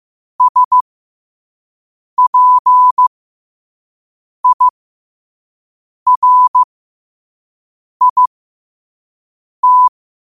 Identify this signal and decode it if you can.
Morse code, heard as a keyed tone, about 15 wpm, spaced as for 5 wpm, 1000 Hz. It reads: SPIRIT
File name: qotdfiles / 5wpm